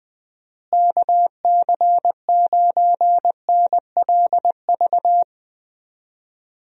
Morse code, heard as a keyed tone, about 20 wpm, 700 Hz. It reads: KC9NLV
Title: USA Call Signs - 6 Characters - Repeated 3x - 20wpm